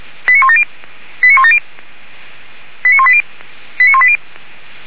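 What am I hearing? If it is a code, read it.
Five-tone selective call (ZVEI), heard as quick runs of five tones, tones of about 80 ms. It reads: 89170 89170 89180 89180
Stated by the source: ZVEI five-tone selective call received by SDR on a fire brigade channel.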